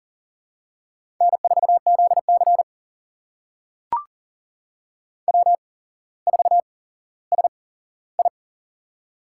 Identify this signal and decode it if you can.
Morse code, heard as a keyed tone, about 40 wpm, spaced as for 17 wpm, 700 Hz. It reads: N4ZC W4SI
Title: USA Call Signs - 4 Characters - 8x Character Spacing - 40wpm